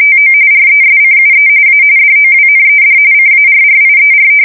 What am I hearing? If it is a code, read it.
RTTY, about 45.45 baud. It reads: TO RESCUE A DOWNED CARRIER
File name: fsk170 / battleshipnj